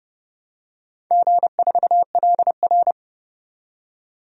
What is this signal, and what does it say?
Morse code, heard as a keyed tone, about 30 wpm, 700 Hz. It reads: G4LR
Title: DXCC-100 Call Signs - 4 Characters - 30wpm